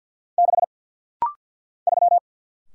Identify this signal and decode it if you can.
Morse code, heard as a keyed tone, about 50 wpm, 700 Hz. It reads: 6 3
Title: Single Letter-Number - Rapid-Fire 50wpm